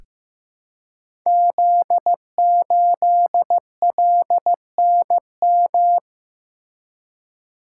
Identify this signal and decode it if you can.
Morse code, heard as a keyed tone, about 15 wpm, 700 Hz. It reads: Z8LNM
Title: Code Groups - 5 Letter:Numbers - 15wpm